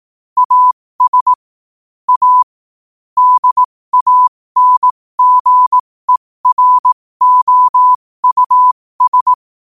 Morse code, heard as a keyed tone, about 18 wpm, spaced as for 15 wpm, 1000 Hz. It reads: AS A DANGEROUS
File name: qotdfiles / 15wpm